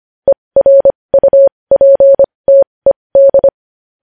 Morse code, heard as a keyed tone, about 25 wpm, spaced as for 20 wpm, 550 Hz. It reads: ERUPTED
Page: news-headlines-in-morse-code-at-20-wpm